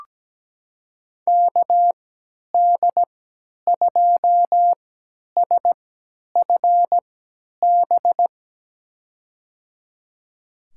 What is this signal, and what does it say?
Morse code, heard as a keyed tone, about 17 wpm, spaced as for 10 wpm, 700 Hz. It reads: KD2SFB
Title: USA Call Signs - 3x Character Spacing - 17wpm